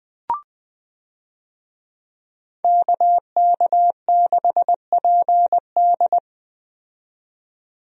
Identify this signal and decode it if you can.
Morse code, heard as a keyed tone, about 20 wpm, 700 Hz. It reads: KK6PD
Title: USA Call Signs - 5 Characters - 20wpm